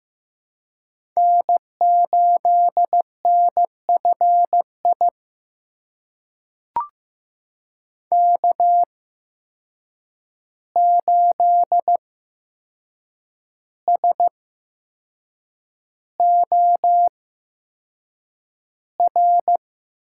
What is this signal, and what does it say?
Morse code, heard as a keyed tone, about 15 wpm, spaced as for 6 wpm, 700 Hz. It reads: N8NFI K8SOR
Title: USA Call Signs - 5 Characters - 8x Character Spacing - 15wpm